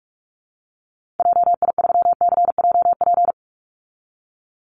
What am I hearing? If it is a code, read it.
Morse code, heard as a keyed tone, about 44 wpm, 700 Hz. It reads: JI3CJP
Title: Call Signs Repeated 2x - 45wpm